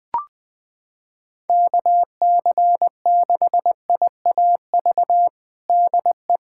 Morse code, heard as a keyed tone, about 21 wpm, 700 Hz. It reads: KC6IAV DE
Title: USA Callsign DE Callsign 20wpm